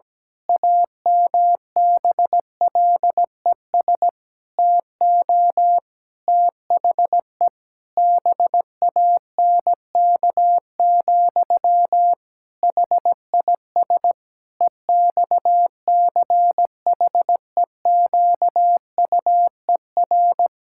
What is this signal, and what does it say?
Morse code, heard as a keyed tone, about 17 wpm, 700 Hz. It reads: AMBLES TO THE BANK, HIS EXCHEQUER
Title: Pangram Challenge v2 - 17wpm